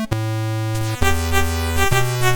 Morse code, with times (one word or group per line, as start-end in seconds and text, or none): none